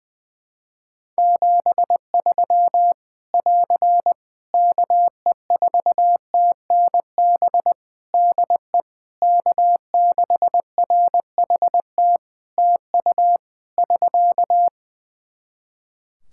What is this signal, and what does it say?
Morse code, heard as a keyed tone, about 21 wpm, 700 Hz. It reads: 73 <AR> KE4TNB DE K6RHT TU <SK>
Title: Practice QSO - Exchange 4 - Ending - 20wpm